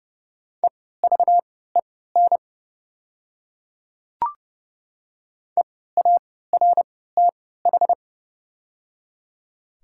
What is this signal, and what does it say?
Morse code, heard as a keyed tone, about 30 wpm, spaced as for 17 wpm, 700 Hz. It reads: EVEN EARTH
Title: 3x Character Spacing - Top 300 Words - 30wpm